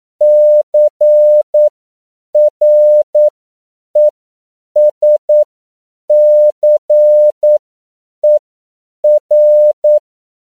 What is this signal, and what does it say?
Morse code, heard as a keyed tone, about 9 wpm, spaced as for 7 wpm, 600 Hz. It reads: CRESCER
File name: CW / 10PPM